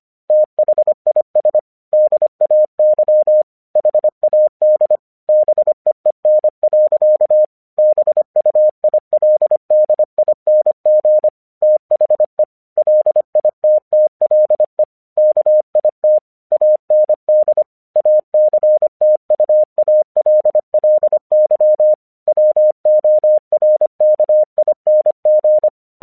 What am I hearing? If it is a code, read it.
Morse code, heard as a keyed tone, about 25 wpm, 610 Hz. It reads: THIS DAY HAD BEEN. BUILDING THE LITTLE KIT AND ACTUALLY WORKING
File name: 1 English 25wpm / AQC32